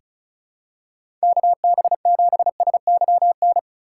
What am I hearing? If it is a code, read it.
Morse code, heard as a keyed tone, about 36 wpm, 700 Hz. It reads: KB7SYN
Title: USA Call Signs - 4x Character Spacing - 35wpm